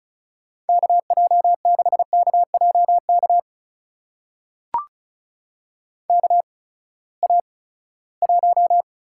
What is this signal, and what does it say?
Morse code, heard as a keyed tone, about 34 wpm, spaced as for 20 wpm, 700 Hz. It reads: KJ6KJK KA1
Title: USA Call Signs - 6 Characters - 8x Character Spacing - 35wpm